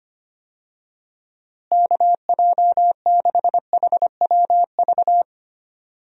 Morse code, heard as a keyed tone, about 25 wpm, 700 Hz. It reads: KJ6HWV
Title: USA Call Signs - 6 Characters - Repeated 2x - 25wpm